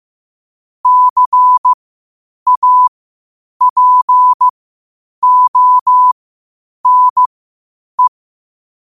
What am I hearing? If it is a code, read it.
Morse code, heard as a keyed tone, about 15 wpm, spaced as for 8 wpm, 1000 Hz. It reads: CAPONE
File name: qotdfiles / 8wpm